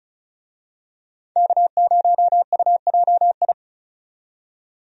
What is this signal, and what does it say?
Morse code, heard as a keyed tone, about 36 wpm, 700 Hz. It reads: K0UJI
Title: USA Call Signs - 5 Characters - 35wpm